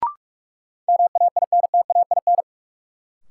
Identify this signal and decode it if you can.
Morse code, heard as a keyed tone, about 45 wpm, 700 Hz. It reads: MAINTAIN
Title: Sentences from Top 1000 Words - Review Part 5 - New Word Drill - 45wpm